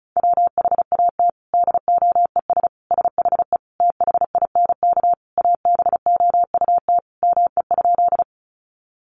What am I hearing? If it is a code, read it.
Morse code, heard as a keyed tone, about 34 wpm, 700 Hz. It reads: WHAT DOES SHE THINK ABOUT ME?
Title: Sentences from Top 200 Words 35wpm